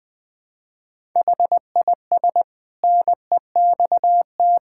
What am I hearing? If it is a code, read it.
Morse code, heard as a keyed tone, about 20 wpm, 700 Hz. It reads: HIS NEXT
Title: Sentences from Top 1000 Words - Review Part 5 - 20wpm